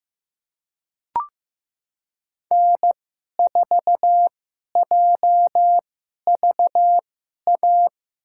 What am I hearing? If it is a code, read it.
Morse code, heard as a keyed tone, about 16 wpm, spaced as for 11 wpm, 700 Hz. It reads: N4JVA
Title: USA Call Signs - 5 Characters - 2x Character Spacing - 15wpm